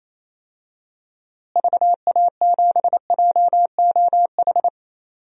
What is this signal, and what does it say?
Morse code, heard as a keyed tone, about 28 wpm, 700 Hz. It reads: VA7JOH
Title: DXCC-100 Call Signs - 6 Characters - Repeated 2x - 28wpm